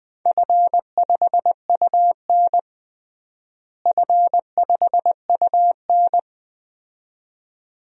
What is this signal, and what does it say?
Morse code, heard as a keyed tone, about 21 wpm, 700 Hz. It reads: F5UN F5UN
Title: DXCC-100 Call Signs - 4 Characters - Repeated 2x - 20wpm